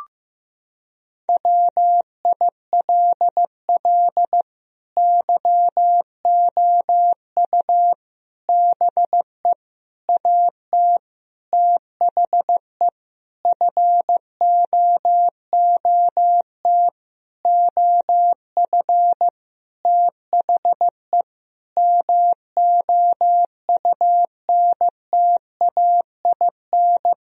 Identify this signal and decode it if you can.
Morse code, heard as a keyed tone, about 16 wpm, 700 Hz. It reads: WILL YOU BE AT THE FOOT OF THE MOUNTAIN
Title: Sentences from Top 1000 Words - Review Part 3 - 15wpm